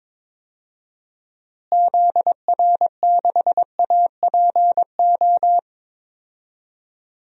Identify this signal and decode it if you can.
Morse code, heard as a keyed tone, about 22 wpm, 700 Hz. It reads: ZR6APO
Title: DXCC-100 Call Signs - 6 Characters - Repeated 3x - 22wpm